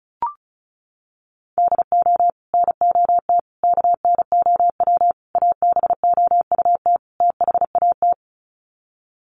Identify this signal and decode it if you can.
Morse code, heard as a keyed tone, about 34 wpm, 700 Hz. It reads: DO NOT KNOW ABOUT THAT
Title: Sets of 5 Words Encore2 35wpm